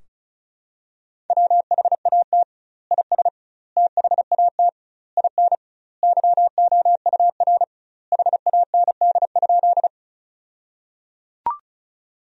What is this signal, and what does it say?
Morse code, heard as a keyed tone, about 36 wpm, 700 Hz. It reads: WHAT IS THAT IN YOUR HAND?
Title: 2x Word-Spacing - Sentences from Top 200 Words - 35wpm